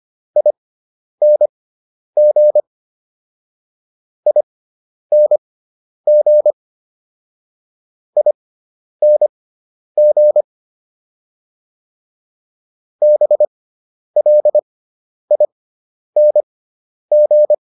Morse code, heard as a keyed tone, about 25 wpm, spaced as for 10 wpm, 600 Hz. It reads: ING ING ING BLING
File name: ING5 10